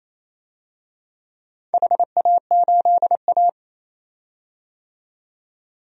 Morse code, heard as a keyed tone, about 28 wpm, 700 Hz. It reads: HA8A
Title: DXCC-100 Call Signs - 4 Characters - 28wpm